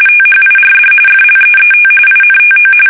RTTY, about 45.45 baud. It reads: THIRD FLEET INTO
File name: fsk850 / battleshipnj